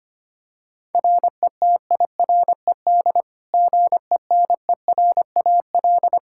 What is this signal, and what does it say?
Morse code, heard as a keyed tone, about 25 wpm, 700 Hz. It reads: RETIRED GENERAL
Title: Practice QSO - Exchange 3 - More Info - 25wpm